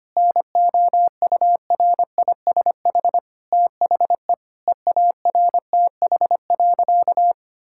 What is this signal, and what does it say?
Morse code, heard as a keyed tone, about 25 wpm, 700 Hz. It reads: NOURISH THE EARTH.
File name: practice / 25wpm / Difficult Sentences with Hints - 25wpm